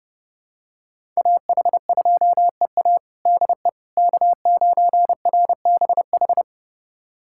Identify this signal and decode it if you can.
Morse code, heard as a keyed tone, about 32 wpm, 700 Hz. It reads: AH2EA DE K9RBH
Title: USA Callsign DE Callsign - 5 Characters - 30wpm